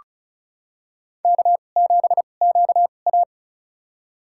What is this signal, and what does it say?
Morse code, heard as a keyed tone, about 34 wpm, spaced as for 25 wpm, 700 Hz. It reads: K7QA
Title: USA Call Signs - 2x Character Spacing - 35wpm